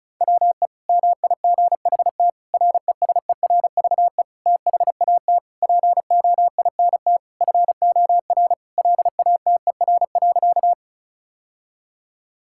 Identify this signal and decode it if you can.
Morse code, heard as a keyed tone, about 36 wpm, 700 Hz. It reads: WE MIGHT RESERVE THAT POINT FOR LATER.
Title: Sentences from Top 1000 Words - Review Part 6 - 35wpm